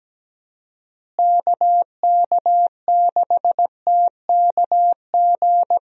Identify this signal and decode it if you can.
Morse code, heard as a keyed tone, about 17 wpm, 700 Hz. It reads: KK6TKG
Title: USA Call Signs - 6 Characters - Repeated 3x - 17wpm